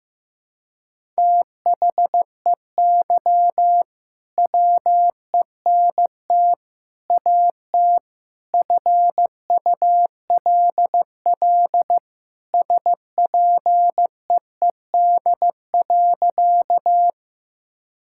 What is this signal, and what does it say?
Morse code, heard as a keyed tone, about 15 wpm, 700 Hz. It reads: THEY WENT AT FULL SPEED.
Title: Sentences from Top 1000 Words - Review Part 5 - 15wpm